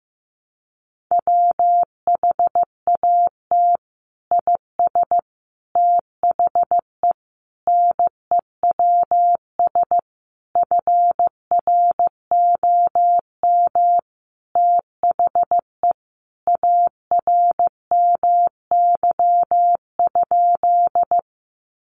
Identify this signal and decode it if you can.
Morse code, heard as a keyed tone, about 15 wpm, 700 Hz. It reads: WHAT IS THE NEWS FROM THE ARMY?
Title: Sentences from Top 800 Words 15wpm